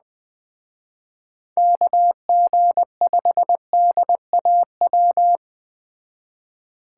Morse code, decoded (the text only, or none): KG5DAW